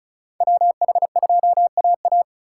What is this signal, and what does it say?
Morse code, heard as a keyed tone, about 34 wpm, 700 Hz. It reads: WH2AA
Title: USA Call Signs - 5 Characters - 2x Character Spacing - 35wpm